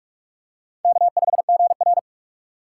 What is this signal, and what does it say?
Morse code, heard as a keyed tone, about 44 wpm, 700 Hz. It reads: K5GR
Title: USA Call Signs - 3x Character Spacing - 45wpm